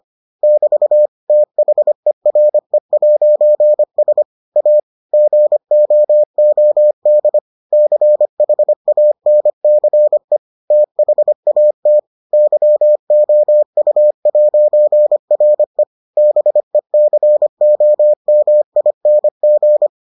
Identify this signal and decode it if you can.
Morse code, heard as a keyed tone, about 25 wpm, 600 Hz. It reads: <BT> THERE'S A GOOD CHANCE THAT YOU'RE BECOMING